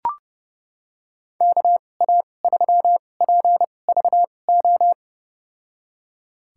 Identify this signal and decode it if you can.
Morse code, heard as a keyed tone, about 32 wpm, spaced as for 22 wpm, 700 Hz. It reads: KA3PVO
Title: USA Call Signs - 6 Characters - 2x Character Spacing - 30wpm